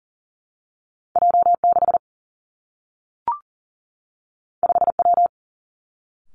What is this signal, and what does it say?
Morse code, heard as a keyed tone, about 42 wpm, 700 Hz. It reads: J 6 5 W
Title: Two Challenging Letter-Number Combinations 40wpm